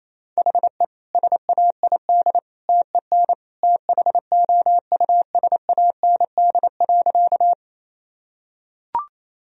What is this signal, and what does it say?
Morse code, heard as a keyed tone, about 28 wpm, 700 Hz. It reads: HE SAID TEN THOUSAND.
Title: Sentences from Top 1000 Words - Review Part 3 - 28wpm